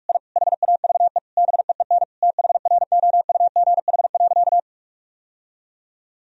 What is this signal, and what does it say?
Morse code, heard as a keyed tone, about 46 wpm, 700 Hz. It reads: I HAVE BEEN THROUGH.
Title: Sentences from Top 1000 Words - Review Part 1 - 45wpm